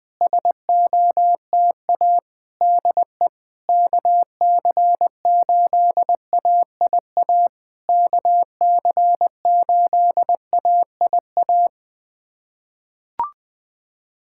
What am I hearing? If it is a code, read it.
Morse code, heard as a keyed tone, about 20 wpm, 690 Hz. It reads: SOTA DE KC8AIA KC8AIA
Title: SOTA QSO Practice 20wpm